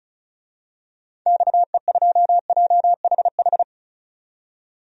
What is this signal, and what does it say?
Morse code, heard as a keyed tone, about 36 wpm, 700 Hz. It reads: XE2JHH
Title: DXCC-100 Call Signs - 6 Characters - 35wpm